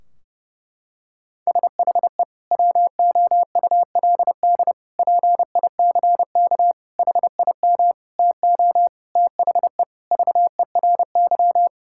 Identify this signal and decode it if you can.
Morse code, heard as a keyed tone, about 32 wpm, 700 Hz. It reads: SHE WOULD PICK HIM TO THE VERY
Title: Sentences from Top 1600 Words 30wpm